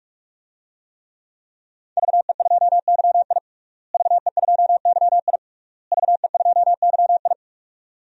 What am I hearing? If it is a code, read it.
Morse code, heard as a keyed tone, about 42 wpm, 700 Hz. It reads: VE2YI VE2YI VE2YI
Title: DXCC-100 Call Signs - 5 Characters - Repeated 3x - 45wpm